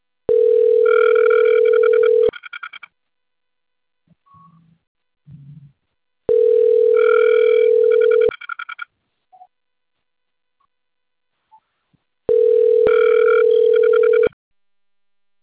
Ringback tone (North America), 440 + 480 Hz.
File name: out- - - - - .52778